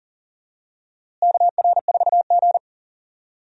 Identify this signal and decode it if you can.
Morse code, heard as a keyed tone, about 40 wpm, 700 Hz. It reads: KR4G
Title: USA Call Signs - 4 Characters - Repeated 3x - 40wpm